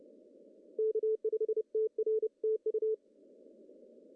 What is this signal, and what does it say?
Morse code, heard as a keyed tone, about 29 wpm, 440 Hz. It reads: K5TR TU
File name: CW / cqww-cw-k5tr